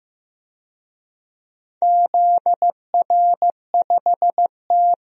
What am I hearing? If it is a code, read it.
Morse code, heard as a keyed tone, about 15 wpm, 700 Hz. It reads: ZR5T
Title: DXCC-100 Call Signs - 4 Characters - 15wpm